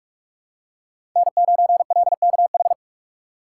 Morse code, heard as a keyed tone, about 42 wpm, 700 Hz. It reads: N9LKH
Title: USA Call Signs - 5 Characters - 3x Character Spacing - 45wpm